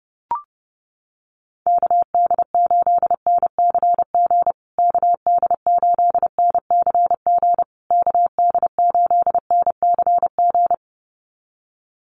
Morse code, heard as a keyed tone, about 30 wpm, 700 Hz. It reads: KD8NCG KD8NCG KD8NCG
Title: USA Call Signs Repeated 3x 30wpm